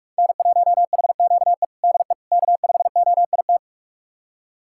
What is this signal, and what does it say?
Morse code, heard as a keyed tone, about 41 wpm, 700 Hz. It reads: N1HQE DE K5OIT
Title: USA Callsign DE Callsign - 5 Characters - 45wpm